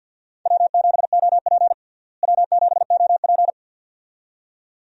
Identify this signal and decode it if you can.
Morse code, heard as a keyed tone, about 52 wpm, 700 Hz. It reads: W7OP W7OP
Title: USA Call Signs - 4 Characters - Repeated 2x - 50wpm